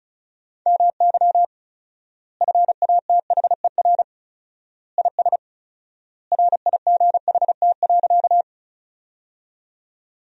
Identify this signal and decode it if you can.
Morse code, heard as a keyed tone, about 36 wpm, 700 Hz. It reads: MY FATHER IS RIGHT.
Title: 4x Word-Spacing - Sentences from Top 300 Words - 35wpm